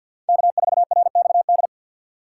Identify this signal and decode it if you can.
Morse code, heard as a keyed tone, about 51 wpm, 700 Hz. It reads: K4RXD
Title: USA Call Signs - 5 Characters - 3x Character Spacing - 50wpm